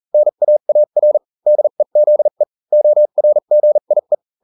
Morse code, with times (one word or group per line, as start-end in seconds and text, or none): 0.1-1.2 NAAR
1.5-2.4 DEZE
2.7-4.1 ORGIE